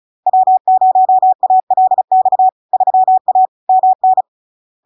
Morse code, heard as a keyed tone, about 36 wpm, 750 Hz. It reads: W0ALX 3A MN
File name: LARK-FD-2022-Log-in-CW-35wpm-750-Hz